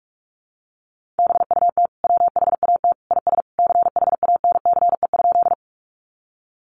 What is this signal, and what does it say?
Morse code, heard as a keyed tone, about 44 wpm, 700 Hz. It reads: BUT WHAT IS CHANCE?
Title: Sentences from Top 800 Words 45wpm